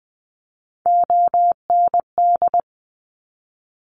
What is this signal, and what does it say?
Morse code, heard as a keyed tone, about 20 wpm, 700 Hz. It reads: OND
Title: Top Three Letter Combinations in Words 20wpm